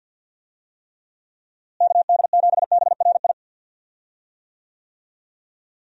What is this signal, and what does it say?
Morse code, heard as a keyed tone, about 52 wpm, 700 Hz. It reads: KD7BRI
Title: USA Call Signs - 6 Characters - Repeated 3x - 50wpm